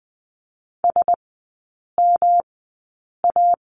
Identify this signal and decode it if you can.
Morse code, heard as a keyed tone, about 20 wpm, spaced as for 8 wpm, 700 Hz. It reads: SMA
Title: 114 Lesson 19e - Letter w - Three Characters 20wpm